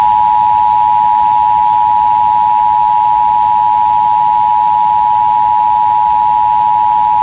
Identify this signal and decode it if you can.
Emergency Alert System (SAME) tones (the 853 and 960 Hz attention tone). It is heard from 0.8 to 6.8 s.